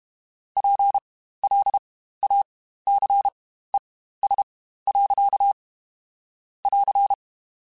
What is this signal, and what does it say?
Morse code, heard as a keyed tone, about 32 wpm, spaced as for 15 wpm, 800 Hz. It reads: PLACES. <AR>
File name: Spaced / Chuckles Spaced-mp3s / MiddleAge sp @30